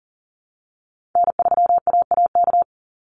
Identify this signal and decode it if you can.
Morse code, heard as a keyed tone, about 37 wpm, 700 Hz. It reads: N3AAK
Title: Call Signs Repeated 3x - 40wpm